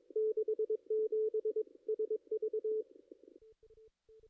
Morse code, heard as a keyed tone, about 20 wpm, 410 Hz. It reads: 67 SV
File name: contests / ss-cw-k5tr-op-n5rz -2300